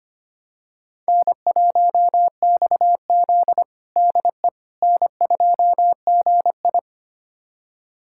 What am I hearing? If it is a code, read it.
Morse code, heard as a keyed tone, about 25 wpm, 700 Hz. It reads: N1XZ DE N2GI
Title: USA Callsign DE Callsign - 4 Characters - 25wpm